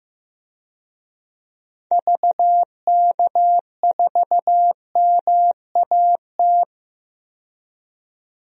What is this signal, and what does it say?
Morse code, heard as a keyed tone, about 15 wpm, 700 Hz. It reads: VK4MAT
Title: DXCC-100 Call Signs - 6 Characters - 15wpm